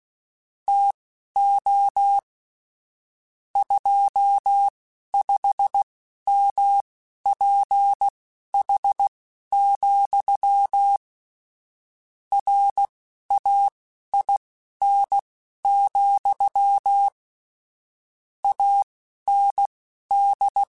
Morse code, heard as a keyed tone, about 16 wpm, spaced as for 11 wpm, 800 Hz. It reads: TO 25MPH, RAIN, AND
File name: Weather Sets / Weather Sets-mp3s / Weather 5 5 @10